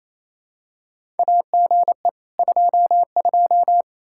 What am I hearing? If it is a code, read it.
Morse code, heard as a keyed tone, about 28 wpm, 700 Hz. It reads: AGE 22
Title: Practice QSO - Exchange 3 - More Info - 28wpm